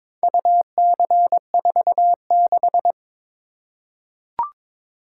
Morse code, heard as a keyed tone, about 22 wpm, 700 Hz. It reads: UC46
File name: practice / 22wpm / Four Letter-Number Combinations - 22wpm